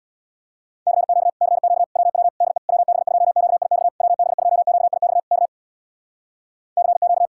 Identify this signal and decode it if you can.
Morse code, heard as a keyed tone, about 64 wpm, 700 Hz. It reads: CQ CQ WW DE WB2QIX WB2QIX K CQ